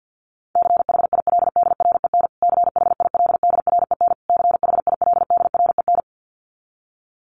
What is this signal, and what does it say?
Morse code, heard as a keyed tone, about 50 wpm, 700 Hz. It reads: CHILDREN CHILDREN CHILDREN
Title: Top 500 Words Repeated 50wpm